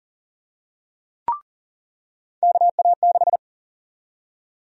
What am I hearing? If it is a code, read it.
Morse code, heard as a keyed tone, about 40 wpm, 700 Hz. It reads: KA6
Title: USA Call Sign Prefixes - 40wpm